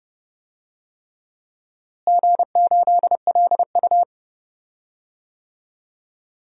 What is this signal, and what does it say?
Morse code, heard as a keyed tone, about 30 wpm, 700 Hz. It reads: G8LU
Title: DXCC-100 Call Signs - 4 Characters - Repeated 3x - 30wpm